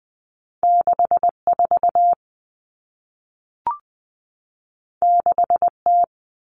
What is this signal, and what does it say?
Morse code, heard as a keyed tone, about 20 wpm, 700 Hz. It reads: 64 6T
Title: Common Cut Numbers - 2-Digit 20wpm